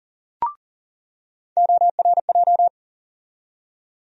Morse code, heard as a keyed tone, about 40 wpm, 700 Hz. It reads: ORJ
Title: USA Call Sign Postfixes - 3 Letters 40wpm